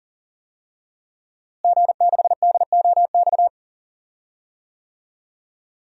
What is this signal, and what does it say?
Morse code, heard as a keyed tone, about 40 wpm, 700 Hz. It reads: G6DOX
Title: DXCC-100 Call Signs - 5 Characters - Repeated 3x - 40wpm